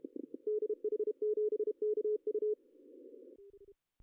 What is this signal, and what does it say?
Morse code, heard as a keyed tone, about 32 wpm, 420 Hz. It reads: DH7KU
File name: CW / cqww-cw-k5tr